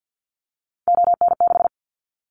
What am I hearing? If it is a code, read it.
Morse code, heard as a keyed tone, about 50 wpm, 700 Hz. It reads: ON6
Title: Callsign Prefixes for Top 50 DXCC - 50wpm